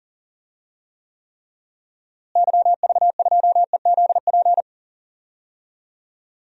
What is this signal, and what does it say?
Morse code, heard as a keyed tone, about 37 wpm, 700 Hz. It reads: YV2EZP
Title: DXCC-100 Call Signs - 6 Characters - Repeated 3x - 40wpm